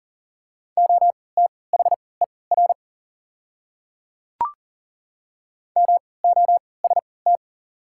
Morse code, heard as a keyed tone, about 38 wpm, spaced as for 22 wpm, 690 Hz. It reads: OTHER MOST
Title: 3x Character Spacing - Top 100 Words - 40wpm